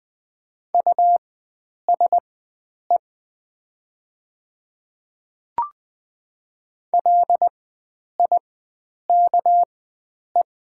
Morse code, heard as a keyed tone, about 20 wpm, spaced as for 9 wpm, 700 Hz. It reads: USE LIKE
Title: 4x Character Spacing - Top 100 Words - 20wpm